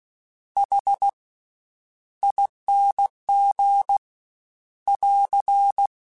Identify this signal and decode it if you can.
Morse code, heard as a keyed tone, about 16 wpm, spaced as for 10 wpm, 800 Hz. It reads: HING <AR>
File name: Special INGs / Special INGs-mp3s / VerySlowToFaster Ing @15